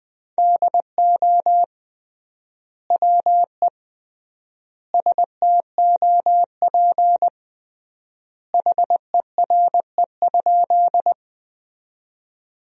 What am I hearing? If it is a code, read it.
Morse code, heard as a keyed tone, about 20 wpm, 700 Hz. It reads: DO WE STOP HERE?
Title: 3x Word-Spacing - Sentences from Top 300 Words - 20wpm